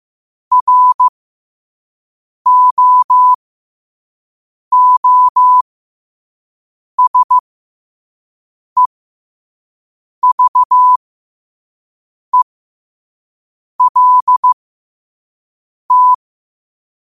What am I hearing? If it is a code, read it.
Morse code, heard as a keyed tone, about 15 wpm, spaced as for 5 wpm, 1000 Hz. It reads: ROOSEVELT
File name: qotdfiles / 5wpm